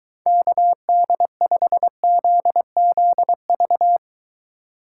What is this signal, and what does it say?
Morse code, heard as a keyed tone, about 22 wpm, 700 Hz. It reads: KD5ZZV